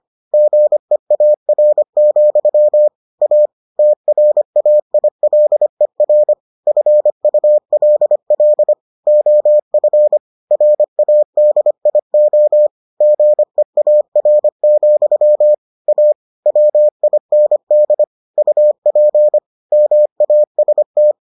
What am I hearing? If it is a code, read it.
Morse code, heard as a keyed tone, about 25 wpm, 600 Hz. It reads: GEAR, A TRAILER FULL OF RADIO GEAR, A WIND UP MAST